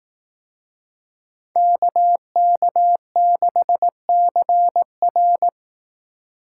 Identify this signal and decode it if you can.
Morse code, heard as a keyed tone, about 18 wpm, 700 Hz. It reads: KK6CR
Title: Speed Racing - USA 2x2 Call Signs 15wpm